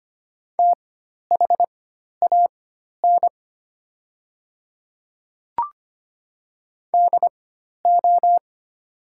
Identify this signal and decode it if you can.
Morse code, heard as a keyed tone, about 25 wpm, spaced as for 12 wpm, 700 Hz. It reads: THAN DO
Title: 4x Character Spacing - Top 100 Words - 25wpm